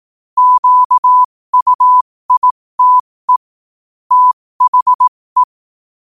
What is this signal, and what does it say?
Morse code, heard as a keyed tone, about 18 wpm, spaced as for 15 wpm, 1000 Hz. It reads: QUITE THE
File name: qotdfiles / 15wpm